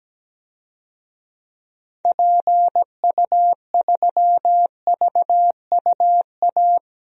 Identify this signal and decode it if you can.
Morse code, heard as a keyed tone, about 17 wpm, 700 Hz. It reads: PU3VUA
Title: DXCC-100 Call Signs - 6 Characters - 17wpm